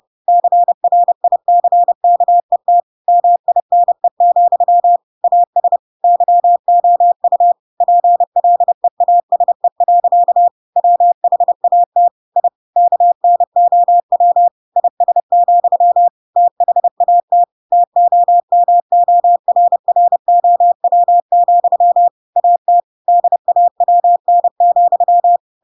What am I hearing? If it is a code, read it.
Morse code, heard as a keyed tone, about 30 wpm, 700 Hz. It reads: CRICKET MINE, AS YOU PLEASE. WHAT I KNOW IS, THAT TOMORROW, AT DAWN,
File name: Pinocchio / Pinocchio ch4 30 30 ik1hge